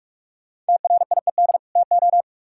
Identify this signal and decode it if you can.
Morse code, heard as a keyed tone, about 60 wpm, spaced as for 53 wpm, 700 Hz. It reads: TRIED TO
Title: Speed-Racing - Sets of 2 Words 45wpm